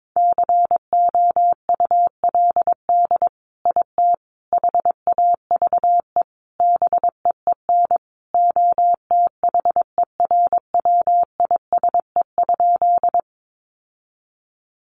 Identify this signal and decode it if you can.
Morse code, heard as a keyed tone, about 22 wpm, 700 Hz. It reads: COULD IT HAVE BEEN OTHERWISE?
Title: Sentences from Top 1200 Words 22wpm